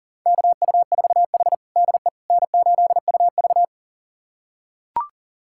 Morse code, heard as a keyed tone, about 40 wpm, 700 Hz. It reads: KU4H DE N8UV
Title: USA Callsign DE Callsign - 4 Characters - 40wpm